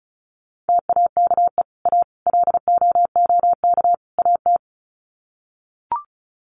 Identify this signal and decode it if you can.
Morse code, heard as a keyed tone, about 34 wpm, 700 Hz. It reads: TAKE A LOOK AT
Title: Sets of 4 Words 35wpm